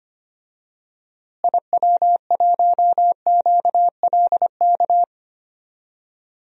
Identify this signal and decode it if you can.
Morse code, heard as a keyed tone, about 25 wpm, 700 Hz. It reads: IW1QLK
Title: DXCC-100 Call Signs - 6 Characters - Repeated 2x - 25wpm